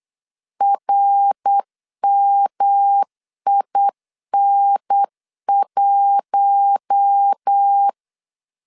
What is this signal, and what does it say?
Morse code, heard as a keyed tone, about 8 wpm, 800 Hz. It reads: RMIN1
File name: LEZIONI / 8 YJ45 / 8 YJ45 8wpm 3214